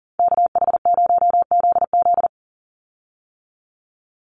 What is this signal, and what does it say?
Morse code, heard as a keyed tone, about 39 wpm, 700 Hz. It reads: KH0ZZ
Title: Call Signs - 40wpm